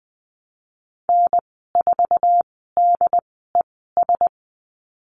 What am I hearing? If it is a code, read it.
Morse code, heard as a keyed tone, about 20 wpm, spaced as for 15 wpm, 700 Hz. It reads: N4DES
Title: 082 Lesson 14g - Number 2 - Call Signs 20wpm